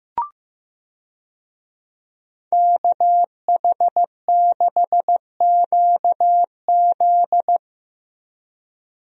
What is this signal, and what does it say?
Morse code, heard as a keyed tone, about 15 wpm, 700 Hz. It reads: KH6QZ
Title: USA Call Signs - 5 Characters - Repeated 2x - 15wpm